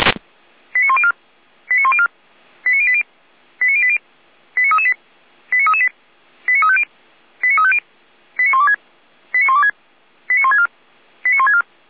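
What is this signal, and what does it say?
Five-tone selective call (ZVEI), heard as quick runs of five tones, tones of about 70 ms. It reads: 89104 89104 89080 89080 89338 89338 89350 89350 89126 89126 89164 89164
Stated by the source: ZVEI five-tone selective call 89025, received on SDR.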